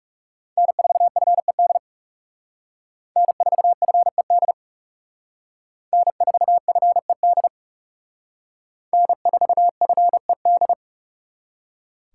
Morse code, heard as a keyed tone, about 37 wpm, 700 Hz. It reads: N4FED N4FED N4FED N4FED
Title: Speed Racing - USA Call Signs 30wpm